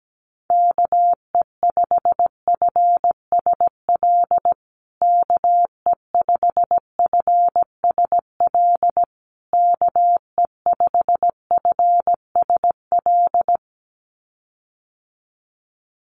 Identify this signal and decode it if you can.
Morse code, heard as a keyed tone, about 17 wpm, 700 Hz. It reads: KE5FSL KE5FSL KE5FSL
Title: USA Call Signs Repeated 3x 17wpm